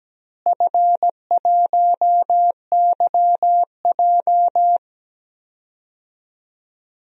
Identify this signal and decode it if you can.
Morse code, heard as a keyed tone, about 17 wpm, 700 Hz. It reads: F1YJ
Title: DXCC-100 Call Signs - 4 Characters - Repeated 2x - 17wpm